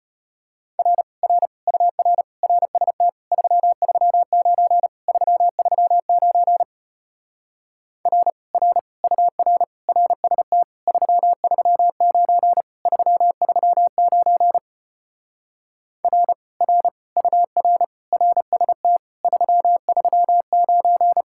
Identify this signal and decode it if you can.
Morse code, heard as a keyed tone, about 33 wpm, 700 Hz. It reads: R R UR RST 339 339 R R UR RST 339 339 R R UR RST 339